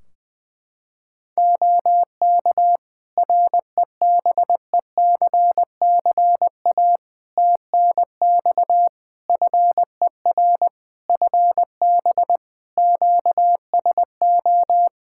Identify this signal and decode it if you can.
Morse code, heard as a keyed tone, about 20 wpm, 700 Hz. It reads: OK REBECCA TNX FER FB QSO
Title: Practice QSO - Exchange 4 - Ending - 20wpm